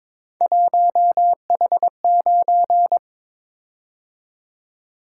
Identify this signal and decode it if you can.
Morse code, heard as a keyed tone, about 22 wpm, 700 Hz. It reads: 1H9
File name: practice / 22wpm / Three Letter-Number Combinations - 22wpm